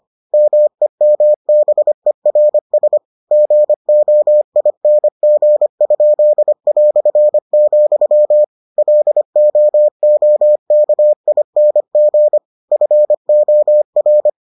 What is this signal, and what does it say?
Morse code, heard as a keyed tone, about 25 wpm, 600 Hz. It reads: MEMBERS GOING?", LOOKING FOR